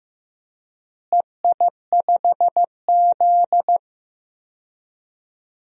Morse code, heard as a keyed tone, about 15 wpm, 700 Hz. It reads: EI5Z
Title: DXCC-100 Call Signs - 4 Characters - Repeated 3x - 15wpm